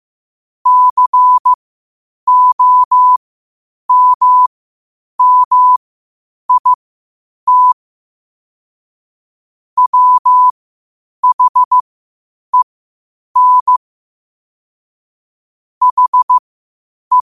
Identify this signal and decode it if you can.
Morse code, heard as a keyed tone, about 15 wpm, spaced as for 8 wpm, 1000 Hz. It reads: COMMIT WHEN HE
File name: qotdfiles / 8wpm